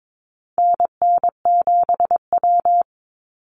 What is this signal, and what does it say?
Morse code, heard as a keyed tone, about 22 wpm, 700 Hz. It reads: NN7W
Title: Call Signs Repeated 3x - 22wpm